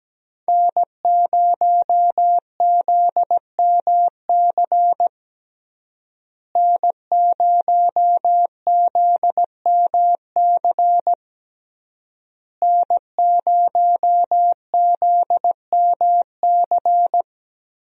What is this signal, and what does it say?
Morse code, heard as a keyed tone, about 17 wpm, 700 Hz. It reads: N0ZMC N0ZMC N0ZMC
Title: USA Call Signs - 5 Characters - Repeated 3x - 17wpm